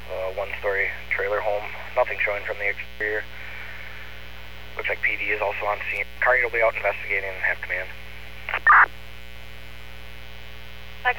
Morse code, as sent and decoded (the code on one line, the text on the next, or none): none